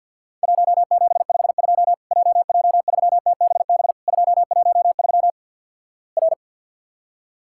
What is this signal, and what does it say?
Morse code, heard as a keyed tone, about 48 wpm, 700 Hz. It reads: 1752 JJ3TBB 213 R
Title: Worked All Europe DX Contest - QTC Exchange - 50wpm